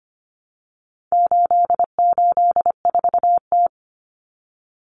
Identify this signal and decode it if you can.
Morse code, heard as a keyed tone, about 25 wpm, 700 Hz. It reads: 884T
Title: Common Cut Numbers - 4-Digit 25wpm